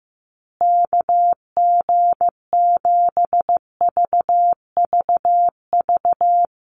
Morse code, heard as a keyed tone, about 15 wpm, 700 Hz. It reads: KG7VVV
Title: USA Call Signs Repeated 3x 15wpm